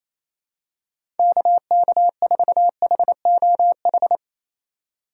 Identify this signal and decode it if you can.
Morse code, heard as a keyed tone, about 28 wpm, 700 Hz. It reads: KK4HOH
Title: USA Call Signs - 6 Characters - 4x Character Spacing - 28wpm